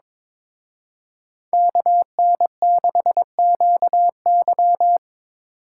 Morse code, heard as a keyed tone, about 26 wpm, 700 Hz. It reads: KN6QY
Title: Speed Racing - USA 2x2 Call Signs 22wpm